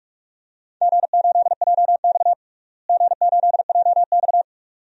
Morse code, heard as a keyed tone, about 46 wpm, 700 Hz. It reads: G8JX G8JX
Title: DXCC-100 Call Signs - 4 Characters - Repeated 2x - 45wpm